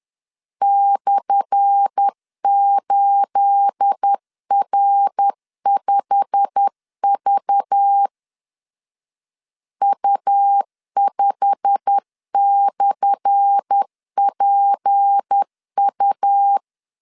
Morse code, heard as a keyed tone, about 10 wpm, 800 Hz. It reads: /8R5V U5/PU
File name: LEZIONI / 10 SIGNS / 10A SLASH 10wpm 2808